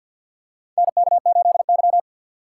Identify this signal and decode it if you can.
Morse code, heard as a keyed tone, about 52 wpm, 700 Hz. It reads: NK8Y
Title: USA Call Signs - 4 Characters - 2x Character Spacing - 50wpm